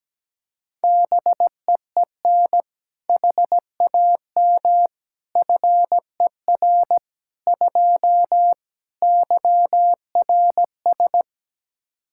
Morse code, heard as a keyed tone, about 17 wpm, 700 Hz. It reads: BEEN HAM FER 2 YRS
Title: K1USN SST QSO Practice 17wpm